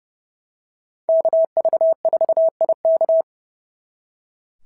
Morse code, heard as a keyed tone, about 30 wpm, 690 Hz. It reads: KV4IK
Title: SOTA QSO Practice 30wpm